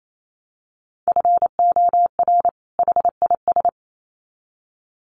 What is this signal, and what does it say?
Morse code, heard as a keyed tone, about 28 wpm, 700 Hz. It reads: FOR HIS
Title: Sets of 2 Words 28wpm